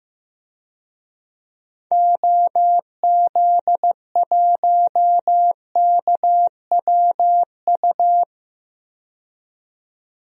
Morse code, heard as a keyed tone, about 16 wpm, 700 Hz. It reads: OZ1KWU
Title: DXCC-100 Call Signs - 6 Characters - 15wpm